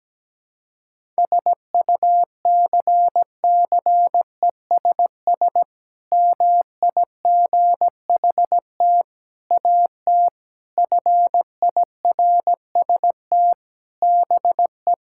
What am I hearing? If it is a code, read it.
Morse code, heard as a keyed tone, about 17 wpm, 700 Hz. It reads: SUCCESS MIGHT AT FIRST BE
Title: Sentences from Top 1000 Words - Review Part 6 - 17wpm